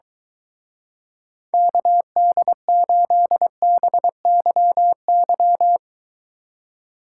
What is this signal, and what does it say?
Morse code, heard as a keyed tone, about 23 wpm, 700 Hz. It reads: KD8BYY